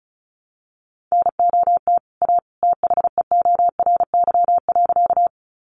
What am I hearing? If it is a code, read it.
Morse code, heard as a keyed tone, about 34 wpm, 700 Hz. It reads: NOT A THEORY.
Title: Sentences from Top 600 Words 35wpm